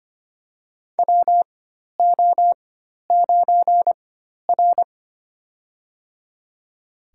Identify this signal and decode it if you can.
Morse code, heard as a keyed tone, about 25 wpm, spaced as for 12 wpm, 700 Hz. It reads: WO9R
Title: USA Call Signs - 4 Characters - 4x Character Spacing - 25wpm